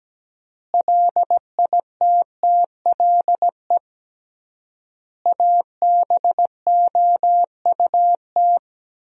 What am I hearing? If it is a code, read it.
Morse code, heard as a keyed tone, about 17 wpm, 700 Hz. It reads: LITTLE ABOUT
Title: 3x Word-Spacing - Sentences from Top 400 Words - 17wpm